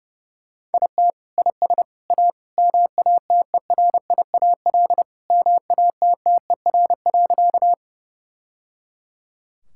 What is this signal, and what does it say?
Morse code, heard as a keyed tone, about 30 wpm, 700 Hz. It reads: IT IS A MATERIAL MATTER.
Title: Sentences from Top 1000 Words - Review Part 5 - 30wpm